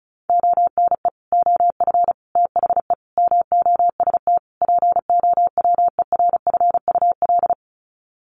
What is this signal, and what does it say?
Morse code, heard as a keyed tone, about 34 wpm, 700 Hz. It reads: ONE OF THE MOST POWERFUL
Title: Sets of 5 Words Encore 35wpm